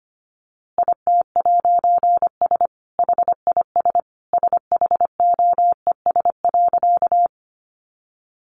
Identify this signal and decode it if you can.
Morse code, heard as a keyed tone, about 25 wpm, 700 Hz. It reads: IT'S HIS SHOES.
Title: A Sound Of Thunder 25wpm